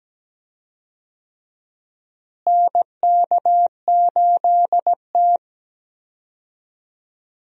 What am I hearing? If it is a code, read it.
Morse code, heard as a keyed tone, about 17 wpm, 700 Hz. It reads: NK8T
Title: USA Call Signs - 4 Characters - Repeated 3x - 17wpm